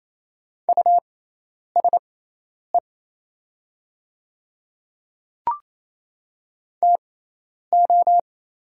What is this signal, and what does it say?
Morse code, heard as a keyed tone, about 27 wpm, spaced as for 10 wpm, 700 Hz. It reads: USE TO
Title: 6x Character Spacing - Top 100 Words - 28wpm